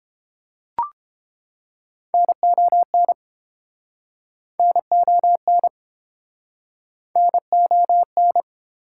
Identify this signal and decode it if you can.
Morse code, heard as a keyed tone, about 32 wpm, 700 Hz. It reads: NON NON NON